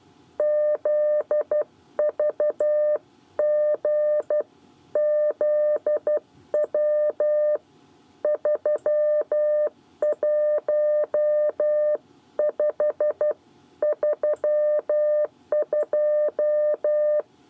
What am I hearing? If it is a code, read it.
Morse code, heard as a keyed tone, about 12 wpm, 600 Hz. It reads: ZVGZW 31532
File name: 8 bis lezione cw 1